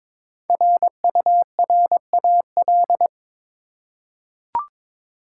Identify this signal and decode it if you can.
Morse code, heard as a keyed tone, about 22 wpm, 690 Hz. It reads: RURAL
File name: 5 Letter Words - First 3 Letters Given - 22wpm